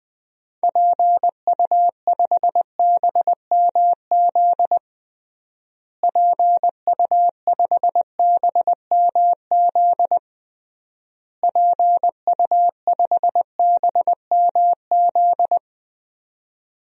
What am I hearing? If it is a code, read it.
Morse code, heard as a keyed tone, about 20 wpm, 700 Hz. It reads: PU5BMZ PU5BMZ PU5BMZ
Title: DXCC-100 Call Signs - 6 Characters - Repeated 3x - 20wpm